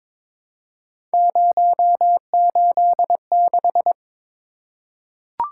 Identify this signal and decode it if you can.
Morse code, heard as a keyed tone, about 22 wpm, 700 Hz. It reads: 086
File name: practice / 22wpm / Three Letter-Number Combinations - 22wpm